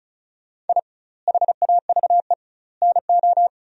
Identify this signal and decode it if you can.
Morse code, heard as a keyed tone, about 36 wpm, 700 Hz. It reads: I HAVE NO
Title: 2x Word-Spacing - Sentences from Top 400 Words - 35wpm